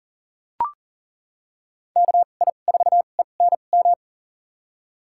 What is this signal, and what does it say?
Morse code, heard as a keyed tone, about 40 wpm, spaced as for 29 wpm, 700 Hz. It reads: KI4ENM
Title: USA Call Signs - 6 Characters - 2x Character Spacing - 40wpm